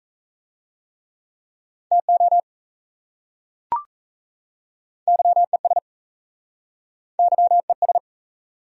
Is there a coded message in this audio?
Yes: Morse code from 1.9 to 8.0 s, heard as a keyed tone, about 39 wpm, 700 Hz. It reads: TO YES YES